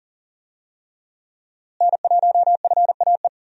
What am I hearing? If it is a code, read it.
Morse code, heard as a keyed tone, about 40 wpm, 700 Hz. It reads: N1FAE
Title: USA Call Signs - 5 Characters - Repeated 3x - 40wpm